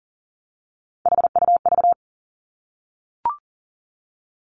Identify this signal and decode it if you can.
Morse code, heard as a keyed tone, about 38 wpm, 700 Hz. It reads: HUV
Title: Three Challenging Letter-Number Combinations 40wpm